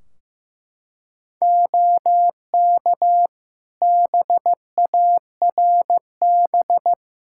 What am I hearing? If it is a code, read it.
Morse code, heard as a keyed tone, about 16 wpm, 700 Hz. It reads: OK BARB
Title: Practice QSO - Exchange 2 - Info - 15wpm